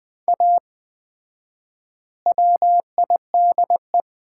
Morse code, heard as a keyed tone, about 20 wpm, 700 Hz. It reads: A WIDE
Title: 4x Word-Spacing - Sets of 4 Words - 20wpm